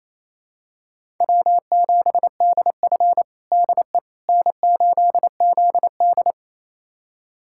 Morse code, heard as a keyed tone, about 28 wpm, 700 Hz. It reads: W7DF DE N8ZD
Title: USA Callsign DE Callsign - 4 Characters - 28wpm